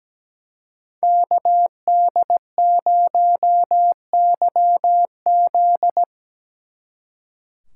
Morse code, heard as a keyed tone, about 20 wpm, 700 Hz. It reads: KD0YZ
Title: Speed Racing - USA 2x2 Call Signs 17wpm